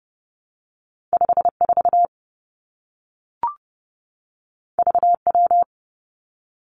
Morse code, heard as a keyed tone, about 30 wpm, 700 Hz. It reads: 5 4 V W
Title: Two Challenging Letter-Number Combinations 30wpm